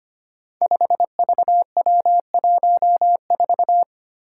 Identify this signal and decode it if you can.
Morse code, heard as a keyed tone, about 25 wpm, 700 Hz. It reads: 5VW14
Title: Five Challenging Letter-Number Combinations 25wpm